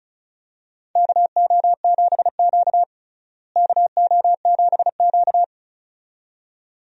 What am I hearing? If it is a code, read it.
Morse code, heard as a keyed tone, about 36 wpm, 700 Hz. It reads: KO7Q KO7Q
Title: USA Call Signs - 4 Characters - Repeated 2x - 35wpm